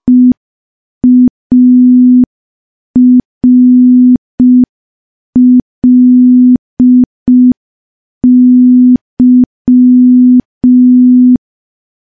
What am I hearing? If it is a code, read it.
Morse code, heard as a keyed tone, about 5 wpm, 250 Hz. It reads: EARLY
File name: morse / morse-training-05WPM